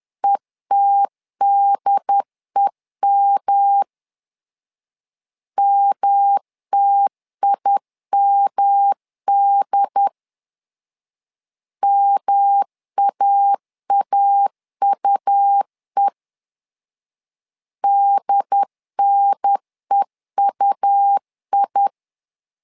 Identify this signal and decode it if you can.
Morse code, heard as a keyed tone, about 10 wpm, 800 Hz. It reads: ETDEM MTIMD MAAUE DNEUI
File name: LEZIONI / 3 MWRO / 3 M 10wpm 1828